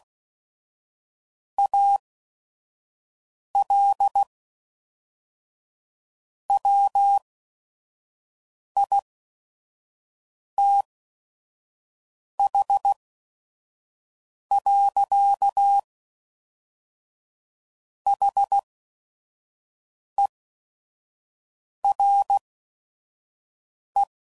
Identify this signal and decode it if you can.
Morse code, heard as a keyed tone, about 16 wpm, spaced as for 5 wpm, 800 Hz. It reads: AL WITH. HERE
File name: CW Code Groups @5WPM / Technical Subjects / AND Gate